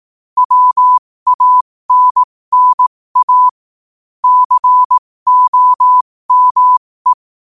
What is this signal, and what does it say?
Morse code, heard as a keyed tone, about 18 wpm, spaced as for 15 wpm, 1000 Hz. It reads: WANNA COME
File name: qotdfiles / 15wpm